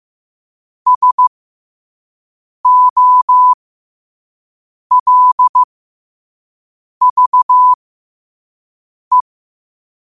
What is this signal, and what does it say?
Morse code, heard as a keyed tone, about 15 wpm, spaced as for 5 wpm, 1000 Hz. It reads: SOLVE
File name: qotdfiles / 5wpm